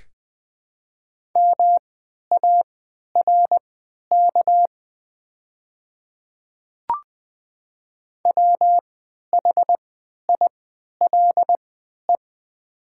Morse code, heard as a keyed tone, about 20 wpm, spaced as for 11 wpm, 700 Hz. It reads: MARK WHILE
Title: 3x Character Spacing - Top 300 Words - 20wpm